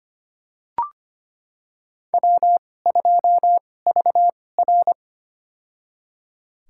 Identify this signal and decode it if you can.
Morse code, heard as a keyed tone, about 25 wpm, spaced as for 18 wpm, 700 Hz. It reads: W2VR
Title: USA Call Signs - 4 Characters - 2x Character Spacing - 25wpm